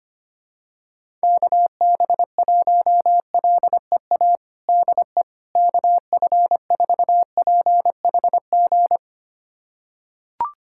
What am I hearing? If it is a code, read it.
Morse code, heard as a keyed tone, about 25 wpm, 700 Hz. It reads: KB1LEA DE KF4PHG
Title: USA Callsign DE Callsign 25wpm